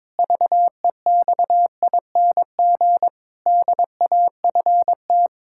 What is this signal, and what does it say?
Morse code, heard as a keyed tone, about 21 wpm, 700 Hz. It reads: VEXING DAFT
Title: Pangram Challenge v2 - 22wpm